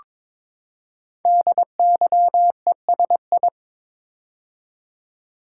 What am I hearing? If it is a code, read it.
Morse code, heard as a keyed tone, about 22 wpm, 700 Hz. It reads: DYESI
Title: Code Groups - 5 Letters - 22wpm